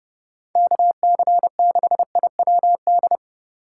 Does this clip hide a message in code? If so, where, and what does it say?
Yes: Morse code from 0.6 to 3.2 s, heard as a keyed tone, about 28 wpm, 700 Hz. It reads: KC6IWD